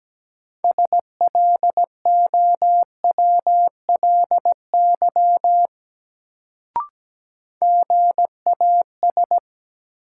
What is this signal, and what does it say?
Morse code, heard as a keyed tone, about 17 wpm, 700 Hz. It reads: SLOWLY GAS
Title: Sentences from Top 1100 Words - New Word Drill - 17wpm